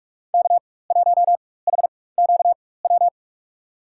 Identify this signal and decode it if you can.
Morse code, heard as a keyed tone, about 46 wpm, spaced as for 21 wpm, 700 Hz. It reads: K1HQW
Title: USA Call Signs - 5 Characters - 4x Character Spacing - 45wpm